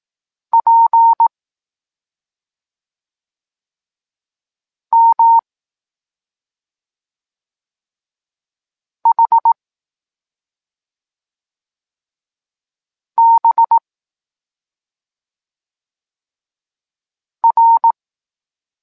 Morse code, heard as a keyed tone, about 18 wpm, 920 Hz. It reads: P M H B R